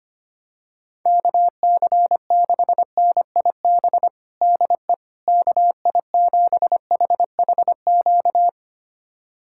Morse code, heard as a keyed tone, about 25 wpm, 700 Hz. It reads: KC6NIB DE KI7HHQ
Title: USA Callsign DE Callsign - 6 Characters - 25wpm